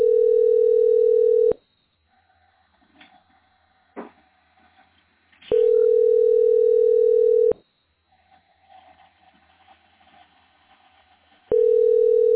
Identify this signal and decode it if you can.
Ringback tone (North America), 440 + 480 Hz.